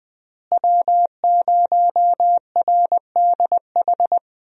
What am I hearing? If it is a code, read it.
Morse code, heard as a keyed tone, about 20 wpm, 700 Hz. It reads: W0RDH
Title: USA Call Signs - 5 Characters - Repeated 2x - 20wpm